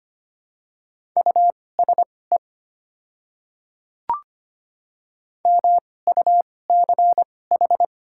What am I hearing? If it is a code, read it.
Morse code, heard as a keyed tone, about 25 wpm, spaced as for 18 wpm, 700 Hz. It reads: USE MUCH
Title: 2x Character Spacing - Top 300 Words - 25wpm